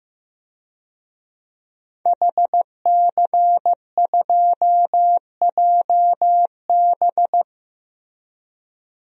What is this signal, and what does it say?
Morse code, heard as a keyed tone, about 15 wpm, 700 Hz. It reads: HC2JB
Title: DXCC-100 Call Signs - 5 Characters - Repeated 2x - 15wpm